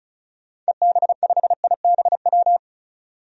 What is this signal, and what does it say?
Morse code, heard as a keyed tone, about 36 wpm, 700 Hz. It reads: EB5IBW
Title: DXCC-100 Call Signs - 6 Characters - 35wpm